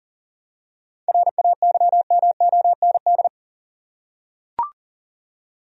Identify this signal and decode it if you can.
Morse code, heard as a keyed tone, about 40 wpm, 700 Hz. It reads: RAYMOND
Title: USA Common Names - 40wpm